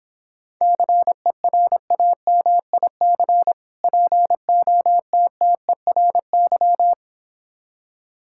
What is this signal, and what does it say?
Morse code, heard as a keyed tone, about 26 wpm, 700 Hz. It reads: CERAMIC POTTERY